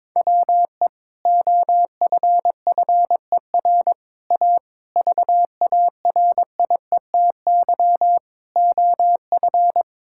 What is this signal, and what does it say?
Morse code, heard as a keyed tone, about 22 wpm, 700 Hz. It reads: WE OFFER A VARIETY OF
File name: practice / 22wpm / Sentences with 3-Word Phrases - 22wpm